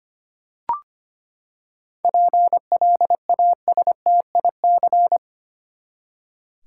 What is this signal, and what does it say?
Morse code, heard as a keyed tone, about 25 wpm, 700 Hz. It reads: PLASTIC
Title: Seven Letter Words - 25wpm